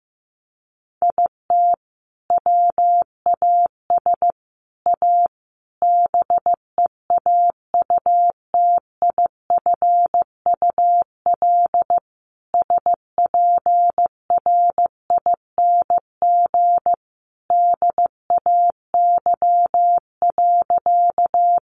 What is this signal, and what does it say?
Morse code, heard as a keyed tone, about 15 wpm, 700 Hz. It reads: IT WAS A BEAUTIFUL SPRING DAY.
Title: Sentences from Top 1200 Words 15wpm